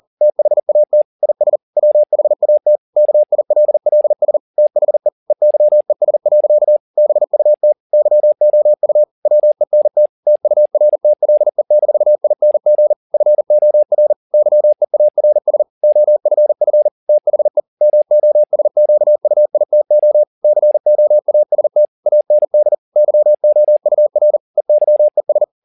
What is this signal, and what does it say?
Morse code, heard as a keyed tone, about 40 wpm, 600 Hz. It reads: THAT IS WHAT KILLS THE EYES. BUT YOU WENT TURTLE-ING FOR YEARS OFF THE MOSQUITO COAST AND YOUR EYES